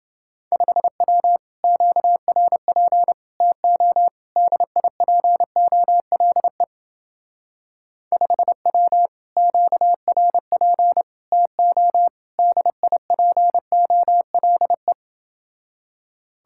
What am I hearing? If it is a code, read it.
Morse code, heard as a keyed tone, about 28 wpm, 700 Hz. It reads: 5W QRP TO DIPOLE 5W QRP TO DIPOLE